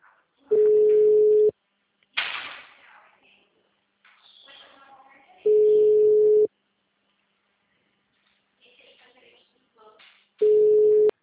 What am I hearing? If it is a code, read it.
Ringback tone (Europe, 425 Hz).
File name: out- -2006- - - .119